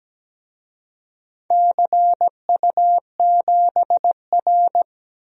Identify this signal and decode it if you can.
Morse code, heard as a keyed tone, about 17 wpm, 700 Hz. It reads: CU7R
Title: DXCC-100 Call Signs - 4 Characters - 17wpm